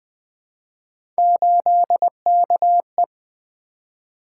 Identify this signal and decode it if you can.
Morse code, heard as a keyed tone, about 20 wpm, 700 Hz. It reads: 8KE
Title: Three Letter-Number Combinations - 20wpm